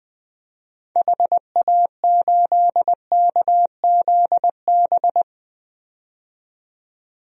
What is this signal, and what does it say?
Morse code, heard as a keyed tone, about 20 wpm, 700 Hz. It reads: HA8KZB
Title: DXCC-100 Call Signs - 6 Characters - 20wpm